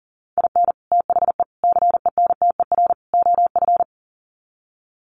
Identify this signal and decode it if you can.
Morse code, heard as a keyed tone, about 40 wpm, 700 Hz. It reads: IN THE CENTER OF
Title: Sets of 4 Words 40wpm